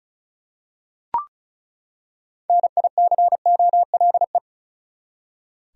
Morse code, heard as a keyed tone, about 36 wpm, 700 Hz. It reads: NICOLE
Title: USA Common Names - 35wpm